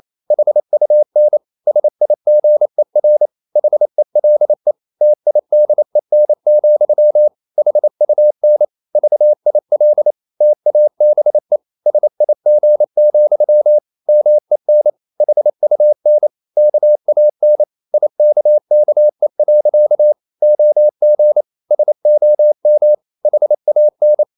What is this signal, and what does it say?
Morse code, heard as a keyed tone, about 28 wpm, 600 Hz. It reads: HUN SIGER HELE TIDEN, HUN VIL TABE SIG, MEN HUN KAN IKKE. OG SOM HAN
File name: Ferie QRQ 28wpm